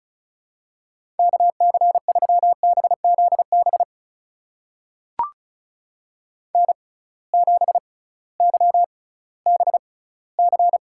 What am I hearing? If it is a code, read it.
Morse code, heard as a keyed tone, about 36 wpm, spaced as for 19 wpm, 700 Hz. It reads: KC3BZB N7YBC
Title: USA Call Signs - 6x Character Spacing - 35wpm